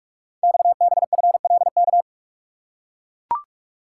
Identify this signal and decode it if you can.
Morse code, heard as a keyed tone, about 44 wpm, 690 Hz. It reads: XBFLK
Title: Code Groups - 5 Letters - 45wpm